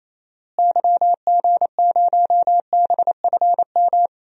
Morse code, heard as a keyed tone, about 28 wpm, 700 Hz. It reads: YG0BFM
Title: DXCC-100 Call Signs - 6 Characters - Repeated 2x - 28wpm